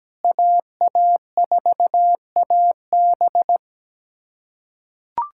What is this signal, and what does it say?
Morse code, heard as a keyed tone, about 17 wpm, 700 Hz. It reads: AA4AB
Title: USA Callsign DE Callsign - 5 Characters - 17wpm